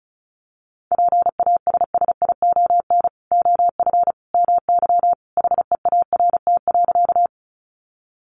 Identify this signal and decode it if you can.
Morse code, heard as a keyed tone, about 34 wpm, 700 Hz. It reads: PASSION OF MY HEART.
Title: Sentences from Top 1300 Words 35wpm